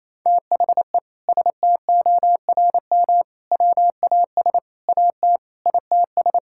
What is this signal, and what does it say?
Morse code, heard as a keyed tone, about 28 wpm, 700 Hz. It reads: THE STORM WAS AT ITS
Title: Sentences from Top 1600 Words 28wpm